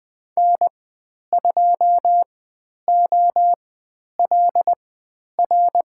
Morse code, heard as a keyed tone, about 20 wpm, spaced as for 10 wpm, 700 Hz. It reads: N2OLR
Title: 109 Lesson 18g - Letter m - Call Signs 20wpm